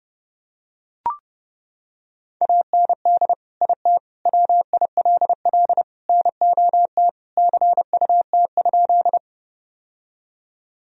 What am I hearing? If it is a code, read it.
Morse code, heard as a keyed tone, about 30 wpm, 700 Hz. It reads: AND IT WILL NOT CUT?
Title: Sentences from Top 1000 Words - Review Part 1 - 30wpm